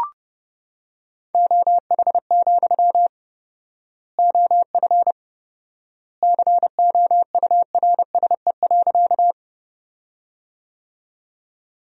Morse code, heard as a keyed tone, about 32 wpm, 700 Hz. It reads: OH, OF COURSE.
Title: 4x Word-Spacing - Sentences from Top 500 Words - 30wpm